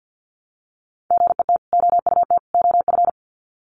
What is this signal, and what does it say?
Morse code, heard as a keyed tone, about 48 wpm, 700 Hz. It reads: GET OUT OF
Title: Sets of 3 Words 50wpm